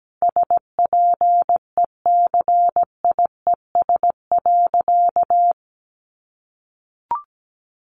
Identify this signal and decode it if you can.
Morse code, heard as a keyed tone, about 17 wpm, 700 Hz. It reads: SPECIES.
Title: A Sound Of Thunder 17wpm